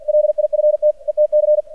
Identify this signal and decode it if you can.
Morse code, heard as a keyed tone, about 50 wpm, 600 Hz. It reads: LIFT ETC
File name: archived / CRSnet